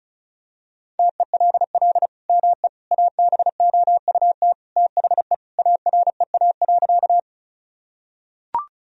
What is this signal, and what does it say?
Morse code, heard as a keyed tone, about 36 wpm, 700 Hz. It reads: TELL ME ABOUT THE AREA.
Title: Sentences from Top 1000 Words - Review Part 2 - 35wpm